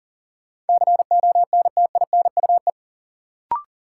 Morse code, heard as a keyed tone, about 40 wpm, 700 Hz. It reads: CONTINUE
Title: Sentences from Top 1100 Words - New Word Drill Alt - 40wpm